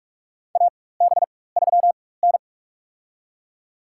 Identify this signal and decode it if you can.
Morse code, heard as a keyed tone, about 47 wpm, spaced as for 21 wpm, 700 Hz. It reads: AB3N
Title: USA Call Signs - 4 Characters - 4x Character Spacing - 45wpm